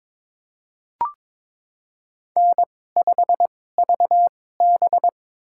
Morse code, heard as a keyed tone, about 22 wpm, spaced as for 16 wpm, 700 Hz. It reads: N5VB
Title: USA Call Signs - 4 Characters - 2x Character Spacing - 22wpm